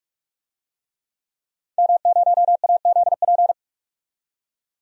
Morse code, heard as a keyed tone, about 44 wpm, 700 Hz. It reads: M0AZP
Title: DXCC-100 Call Signs - 5 Characters - 45wpm